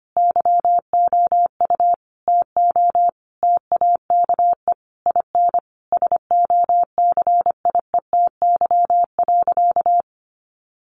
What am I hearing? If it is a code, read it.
Morse code, heard as a keyed tone, about 25 wpm, 700 Hz. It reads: YOU TO TAKE IN SOCIETY.
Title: Sentences from Top 1500 Words 25wpm